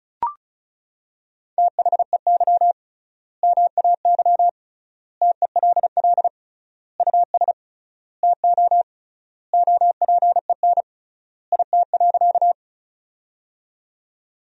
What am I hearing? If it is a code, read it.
Morse code, heard as a keyed tone, about 34 wpm, 700 Hz. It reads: THEY MAY TELL US TO OPEN IT.
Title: 3x Word-Spacing - Sentences from Top 300 Words - 35wpm